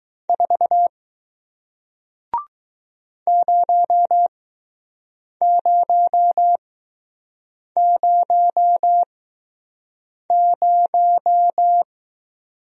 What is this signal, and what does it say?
Morse code, heard as a keyed tone, about 20 wpm, spaced as for 6 wpm, 700 Hz. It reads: 4 0000